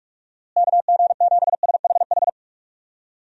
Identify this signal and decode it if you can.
Morse code, heard as a keyed tone, about 46 wpm, 700 Hz. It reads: KG7SHH
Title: USA Call Signs - 6 Characters - Repeated 3x - 45wpm